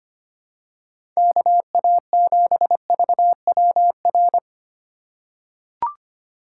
Morse code, heard as a keyed tone, about 25 wpm, 700 Hz. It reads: KA7VWR
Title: USA Call Signs - 6 Characters - 2x Character Spacing - 25wpm